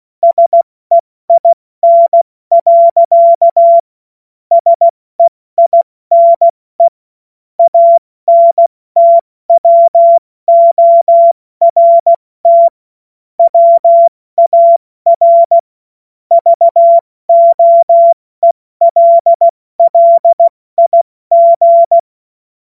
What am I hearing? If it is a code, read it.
Morse code, heard as a keyed tone, about 16 wpm, 680 Hz. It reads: SEIN. SEINE ANTWORT WAR VOELLIG
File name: Morsekurs / Witze / Witz 16. August 70 F80